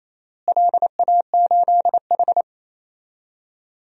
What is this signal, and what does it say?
Morse code, heard as a keyed tone, about 27 wpm, 700 Hz. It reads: LA8H
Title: DXCC-100 Call Signs - 4 Characters - Repeated 3x - 28wpm